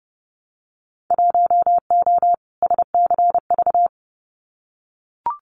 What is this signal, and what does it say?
Morse code, heard as a keyed tone, about 30 wpm, 700 Hz. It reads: 1O SCV
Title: Winter Field Day Exchanges 30wpm